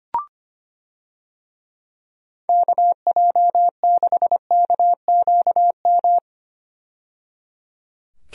Morse code, heard as a keyed tone, about 25 wpm, 700 Hz. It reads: KJ6KQM
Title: USA Call Signs - 6 Characters - 25wpm